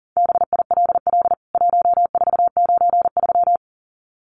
Morse code, heard as a keyed tone, about 37 wpm, 700 Hz. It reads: BILL 1493
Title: CWT Exchange 40wpm